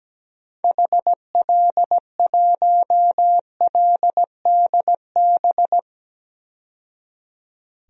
Morse code, heard as a keyed tone, about 17 wpm, 700 Hz. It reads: HL1LDB
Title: DXCC-100 Call Signs - 6 Characters - 17wpm